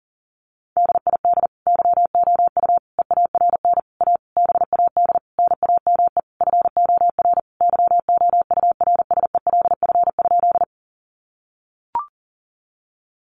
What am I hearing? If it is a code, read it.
Morse code, heard as a keyed tone, about 40 wpm, 700 Hz. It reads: DID YOU EARN A BAD NAME FOR YOURSELF?
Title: Sentences from Top 1500 Words 40wpm